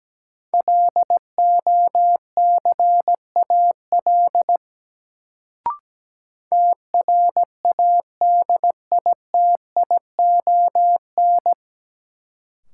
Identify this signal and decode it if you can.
Morse code, heard as a keyed tone, about 17 wpm, 700 Hz. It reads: LOCAL TRADITION
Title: Sentences from Top 1500 Words - New Word Drill Alt - 17wpm